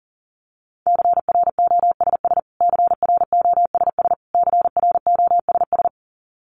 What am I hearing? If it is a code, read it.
Morse code, heard as a keyed tone, about 40 wpm, 700 Hz. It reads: CROSS CROSS CROSS
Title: Top 500 Words Repeated 40wpm